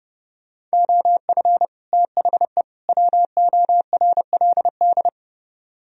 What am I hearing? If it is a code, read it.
Morse code, heard as a keyed tone, about 37 wpm, 700 Hz. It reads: OF THE WORLD
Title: Speed-Racing - Sets of 3 Words 30wpm